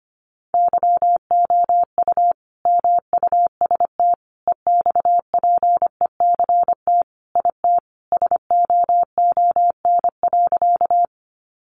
Morse code, heard as a keyed tone, about 25 wpm, 700 Hz. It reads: YOU MUST EXPECT IT SOON.
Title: Sentences from Top 1100 Words 25wpm